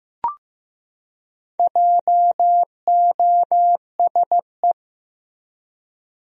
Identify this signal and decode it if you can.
Morse code, heard as a keyed tone, about 16 wpm, 700 Hz. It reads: JOSE
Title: USA Common Names - 15wpm